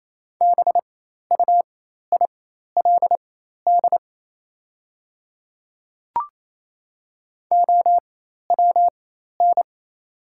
Morse code, heard as a keyed tone, about 28 wpm, spaced as for 13 wpm, 690 Hz. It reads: BUILD OWN
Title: 4x Character Spacing - Top 500 Words - 28wpm